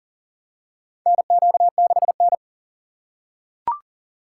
Morse code, heard as a keyed tone, about 40 wpm, 700 Hz. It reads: NQ6N
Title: USA Call Signs - 4x Character Spacing - 40wpm